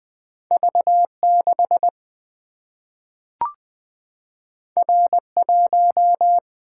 Morse code, heard as a keyed tone, about 20 wpm, 700 Hz. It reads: V 6 R 1
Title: Single-Letter Single-Number 20wpm